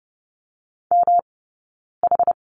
Morse code, heard as a keyed tone, about 30 wpm, 700 Hz. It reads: M H
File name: Lesson 19d - Letter w - Two Characters 30wpm